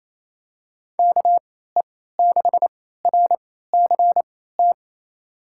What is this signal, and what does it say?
Morse code, heard as a keyed tone, about 28 wpm, spaced as for 16 wpm, 700 Hz. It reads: KE6RCT
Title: USA Call Signs - 3x Character Spacing - 28wpm